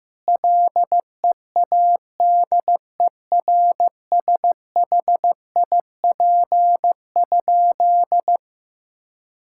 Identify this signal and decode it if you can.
Morse code, heard as a keyed tone, about 15 wpm, 700 Hz. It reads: LEADERSHIP?
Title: Sentences from Top 1600 Words 15wpm